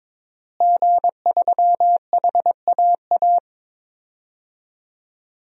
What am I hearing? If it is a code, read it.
Morse code, heard as a keyed tone, about 22 wpm, 700 Hz. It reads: G3HAA
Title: DXCC-100 Call Signs - 5 Characters - Repeated 3x - 22wpm